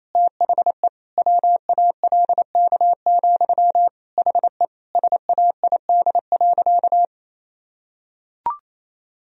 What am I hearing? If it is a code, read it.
Morse code, heard as a keyed tone, about 28 wpm, 700 Hz. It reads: THE WALK, HE SAID.
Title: Sentences from Top 1000 Words - Review Part 1 - 28wpm